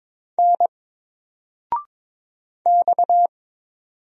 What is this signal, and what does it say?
Morse code, heard as a keyed tone, about 22 wpm, 700 Hz. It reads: N X
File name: Single-Letter-Number-22wpm